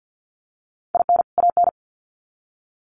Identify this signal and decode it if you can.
Morse code, heard as a keyed tone, about 50 wpm, 700 Hz. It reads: IN AN
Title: Sets of 2 Words 50wpm